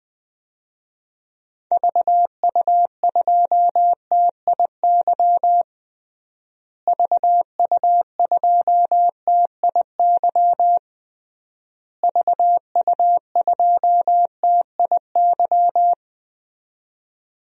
Morse code, heard as a keyed tone, about 20 wpm, 700 Hz. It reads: VU2TIY VU2TIY VU2TIY
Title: DXCC-100 Call Signs - 6 Characters - Repeated 3x - 20wpm